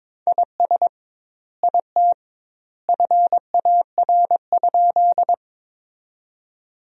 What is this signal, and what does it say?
Morse code, heard as a keyed tone, about 22 wpm, 700 Hz. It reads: IS IT FAR?
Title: 2x Word-Spacing - Sentences from Top 300 Words - 22wpm